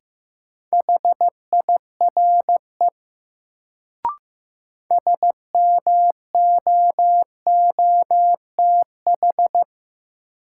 Morse code, heard as a keyed tone, about 15 wpm, 700 Hz. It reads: HIRE SMOOTH
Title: Sentences from Top 1400 Words - New Word Drill Alt - 15wpm